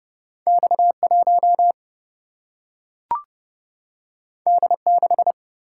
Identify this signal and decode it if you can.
Morse code, heard as a keyed tone, about 31 wpm, 700 Hz. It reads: X1 D6
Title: Single-Letter Single-Number 30wpm